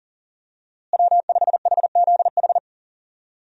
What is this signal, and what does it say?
Morse code, heard as a keyed tone, about 37 wpm, 700 Hz. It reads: W5HZH
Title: USA Call Signs - 5 Characters - Repeated 3x - 40wpm